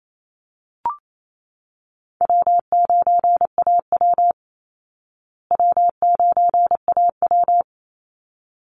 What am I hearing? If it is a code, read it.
Morse code, heard as a keyed tone, about 28 wpm, 700 Hz. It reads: W9AW W9AW
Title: Call Signs Repeated 3x - 28wpm